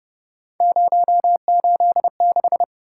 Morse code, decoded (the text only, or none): 086